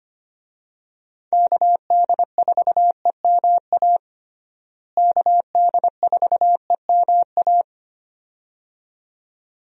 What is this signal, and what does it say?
Morse code, heard as a keyed tone, about 25 wpm, 700 Hz. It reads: KD4EMA KD4EMA
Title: USA Call Signs - 6 Characters - Repeated 2x - 25wpm